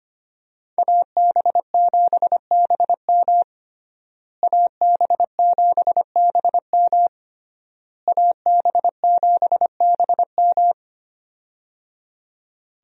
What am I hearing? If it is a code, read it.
Morse code, heard as a keyed tone, about 25 wpm, 700 Hz. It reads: AB7BM AB7BM AB7BM
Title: USA Call Signs - 5 Characters - Repeated 3x - 25wpm